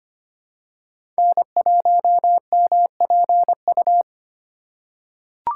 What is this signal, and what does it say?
Morse code, heard as a keyed tone, about 25 wpm, 700 Hz. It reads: N1MPU
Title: USA Call Signs - 2x Character Spacing - 25wpm